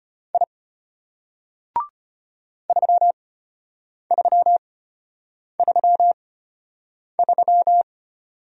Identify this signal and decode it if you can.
Morse code, heard as a keyed tone, about 33 wpm, spaced as for 8 wpm, 700 Hz. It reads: I 3333